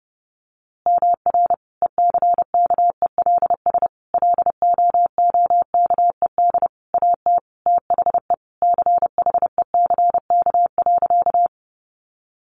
Morse code, heard as a keyed tone, about 30 wpm, 700 Hz. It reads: MR ECKELS LOOKED AT THE CHECK.
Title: A Sound Of Thunder 30wpm